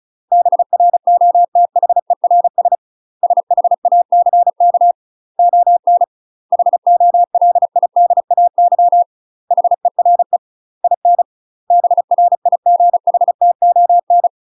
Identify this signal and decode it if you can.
Morse code, heard as a keyed tone, about 36 wpm, 700 Hz. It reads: BROTHERS SHACK ON HOLIDAY HERE IN BRIGHTON
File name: UK MASTER QSOs TRAINING / 35WPM / UK MASTER QSOs 0031